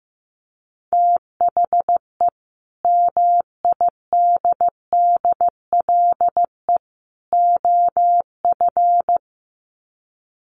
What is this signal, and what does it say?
Morse code, heard as a keyed tone, about 15 wpm, 700 Hz. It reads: THE MIDDLE OF
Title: Sets of 3 Words 15wpm